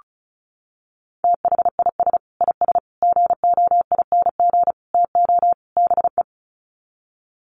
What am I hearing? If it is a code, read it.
Morse code, heard as a keyed tone, about 34 wpm, 700 Hz. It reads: THIS IS GOING TO BE
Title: Sets of 5 Words 35wpm